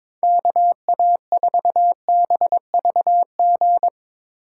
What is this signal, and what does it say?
Morse code, heard as a keyed tone, about 22 wpm, 700 Hz. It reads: KA4BVG
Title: K1USN SST QSO Practice - 22wpm